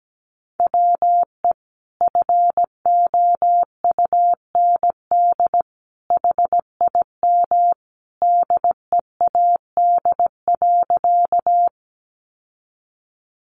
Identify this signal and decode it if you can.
Morse code, heard as a keyed tone, about 17 wpm, 700 Hz. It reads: WE FOUND HIM DEAD.
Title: Sentences from Top 1500 Words 17wpm